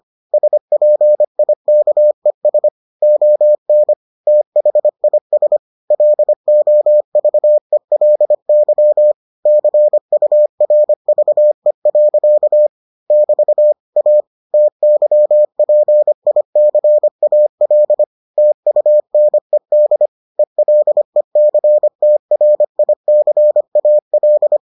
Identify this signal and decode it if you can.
Morse code, heard as a keyed tone, about 25 wpm, 600 Hz. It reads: SPIKES ON THIS LOVELY CURVE. <BT> A TYPICAL TUNED ELECTRICAL